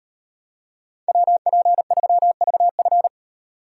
Morse code, heard as a keyed tone, about 32 wpm, 700 Hz. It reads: WP3VF